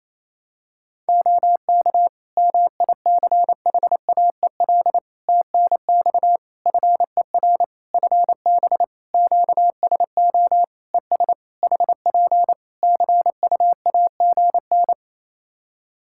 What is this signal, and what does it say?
Morse code, heard as a keyed tone, about 28 wpm, 700 Hz. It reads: OK MICHAEL TNX FER FB QSO ES HP CUAGN
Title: Practice QSO - Exchange 4 - Ending - 28wpm